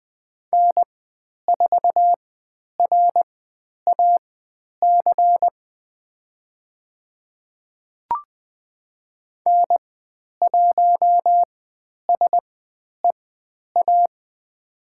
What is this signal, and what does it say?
Morse code, heard as a keyed tone, about 20 wpm, spaced as for 10 wpm, 700 Hz. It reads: N4RAC N1SEA
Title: 102 Lesson 17g - Letter u - Call Signs 20wpm